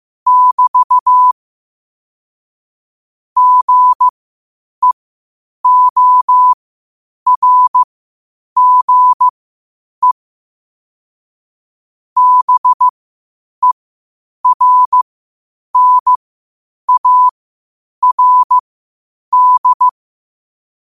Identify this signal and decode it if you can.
Morse code, heard as a keyed tone, about 15 wpm, spaced as for 8 wpm, 1000 Hz. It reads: <BT> GEORGE BERNARD
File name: qotdfiles / 8wpm